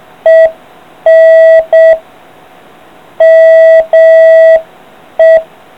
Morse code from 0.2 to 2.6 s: EN